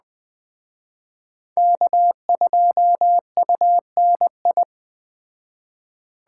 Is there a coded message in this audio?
Yes: Morse code from 1.6 to 4.6 s, heard as a keyed tone, about 19 wpm, 700 Hz. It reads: K2UNI